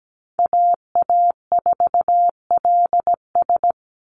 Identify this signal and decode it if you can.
Morse code, heard as a keyed tone, about 17 wpm, 700 Hz. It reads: AA4LS
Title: Call Signs Repeated 2x - 17wpm